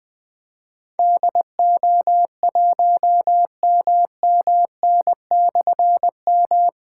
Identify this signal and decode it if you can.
Morse code, heard as a keyed tone, about 20 wpm, 700 Hz. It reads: DO1MMN/M
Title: Difficult Call Signs 20wpm